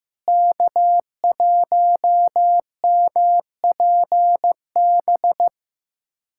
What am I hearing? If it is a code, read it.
Morse code, heard as a keyed tone, about 15 wpm, 700 Hz. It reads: K1MPB
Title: USA Callsign DE Callsign - 5 Characters - 15wpm